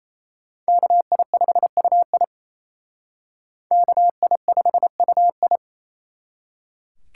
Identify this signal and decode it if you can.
Morse code, heard as a keyed tone, about 35 wpm, 700 Hz. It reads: KI5UI KI5UI
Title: Speed Racing - USA 2x2 Call Signs 28wpm